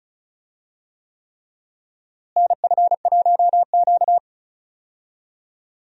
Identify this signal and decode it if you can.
Morse code, heard as a keyed tone, about 36 wpm, 700 Hz. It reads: NF1Q
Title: USA Call Signs - 4 Characters - Repeated 3x - 35wpm